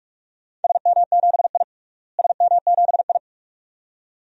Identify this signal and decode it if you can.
Morse code, heard as a keyed tone, about 46 wpm, 700 Hz. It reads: SM7I SM7I
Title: DXCC-100 Call Signs - 4 Characters - Repeated 2x - 45wpm